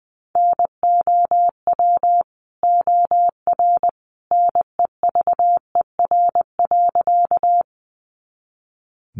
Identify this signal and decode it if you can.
Morse code, heard as a keyed tone, about 20 wpm, 700 Hz. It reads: NOW OR NEVER.
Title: Sentences from Top 300 Words 20wpm